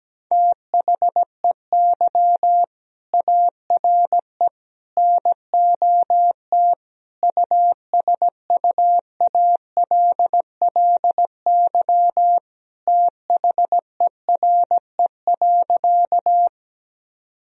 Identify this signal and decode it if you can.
Morse code, heard as a keyed tone, about 17 wpm, 700 Hz. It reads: THEY ARE NOT USUALLY THERE.
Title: Sentences from Top 1000 Words - Review Part 3 - 17wpm